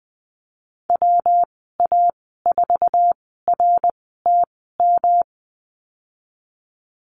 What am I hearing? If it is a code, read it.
Morse code, heard as a keyed tone, about 20 wpm, spaced as for 14 wpm, 700 Hz. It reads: WA4RTM
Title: 161 Lesson 26g - Letter g - Call Signs 20wpm